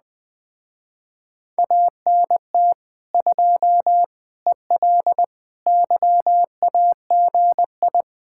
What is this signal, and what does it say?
Morse code, heard as a keyed tone, about 20 wpm, 700 Hz. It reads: ANT 2 EL YAGI
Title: Speed-Racing Common QSO Phrases - 17wpm